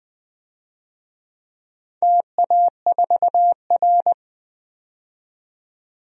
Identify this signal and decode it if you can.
Morse code, heard as a keyed tone, about 20 wpm, 700 Hz. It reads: TA4R
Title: DXCC-100 Call Signs - 4 Characters - 20wpm